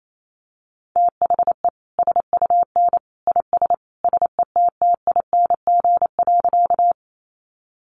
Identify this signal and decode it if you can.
Morse code, heard as a keyed tone, about 28 wpm, 700 Hz. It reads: THE SUN IS SETTING.
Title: Sentences from Top 1300 Words 28wpm